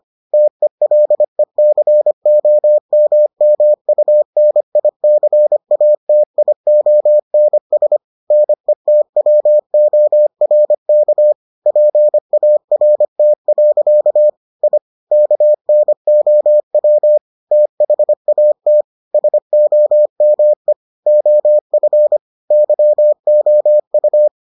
Morse code, heard as a keyed tone, about 25 wpm, 600 Hz. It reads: TELECOMMUNICATIONS NETWORK PART. I KNOW THAT SOME OF YOU